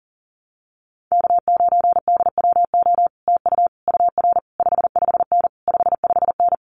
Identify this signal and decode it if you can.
Morse code, heard as a keyed tone, about 40 wpm, 700 Hz. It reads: K9DWO TU UR 55N 55N
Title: Common QSO Phrases Encore 40wpm